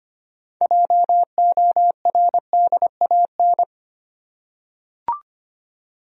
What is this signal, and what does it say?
Morse code, heard as a keyed tone, about 25 wpm, 700 Hz. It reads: JORDAN
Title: USA Common Names - 25wpm